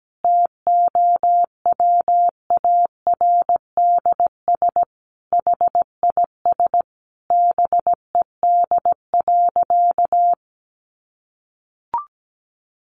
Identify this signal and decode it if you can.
Morse code, heard as a keyed tone, about 17 wpm, 700 Hz. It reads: TOWARDS HIS BED.
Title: Sentences from Top 1100 Words 17wpm